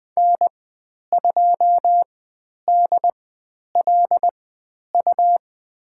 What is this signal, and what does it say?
Morse code, heard as a keyed tone, about 20 wpm, spaced as for 10 wpm, 700 Hz. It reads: N2DLU
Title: 102 Lesson 17g - Letter u - Call Signs 20wpm